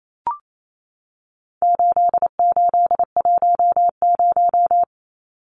Morse code, heard as a keyed tone, about 28 wpm, 700 Hz. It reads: 8810
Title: Numbers - 4-Digit 28wpm